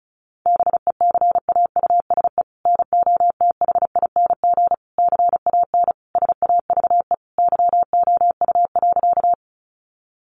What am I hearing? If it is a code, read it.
Morse code, heard as a keyed tone, about 34 wpm, 700 Hz. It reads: BECAUSE NOTHING CAN SAVE YOU.
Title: Sentences from Top 1100 Words 35wpm